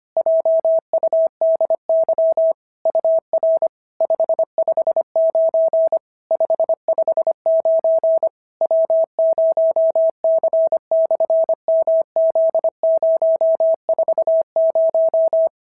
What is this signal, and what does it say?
Morse code, heard as a keyed tone, about 25 wpm, 650 Hz. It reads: JUDY UR 559 559 W0C/MZ040
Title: SOTA QSO Practice 25wpm